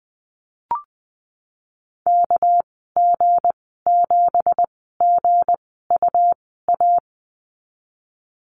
Morse code, heard as a keyed tone, about 20 wpm, spaced as for 14 wpm, 700 Hz. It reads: KG7GUA
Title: 245 Lesson 39g - Letter z - Call Signs 20wpm